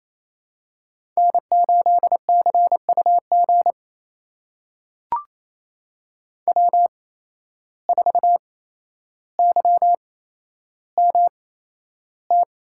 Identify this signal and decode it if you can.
Morse code, heard as a keyed tone, about 27 wpm, spaced as for 12 wpm, 700 Hz. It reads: N8CUG W4YMT
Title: USA Call Signs - 5 Characters - 8x Character Spacing - 28wpm